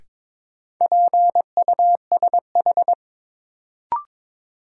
Morse code, heard as a keyed tone, about 23 wpm, 700 Hz. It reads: PUSH
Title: Sentences from Top 1300 Words - New Word Drill - 22wpm